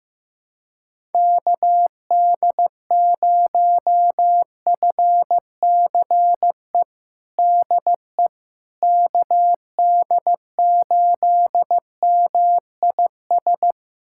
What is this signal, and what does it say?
Morse code, heard as a keyed tone, about 15 wpm, 700 Hz. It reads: KD0FCE DE KD8MIS
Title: USA Callsign DE Callsign 15wpm